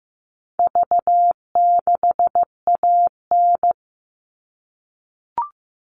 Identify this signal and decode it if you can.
Morse code, heard as a keyed tone, about 15 wpm, 700 Hz. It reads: V6AN
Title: All Cut Numbers - 4-Digit 15wpm